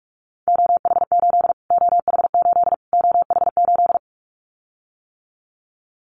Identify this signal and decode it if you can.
Morse code, heard as a keyed tone, about 44 wpm, 700 Hz. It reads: OH8 OH8 OH8
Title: Callsign Prefixes for Top 25 DXCC - Repeated 3x - 45wpm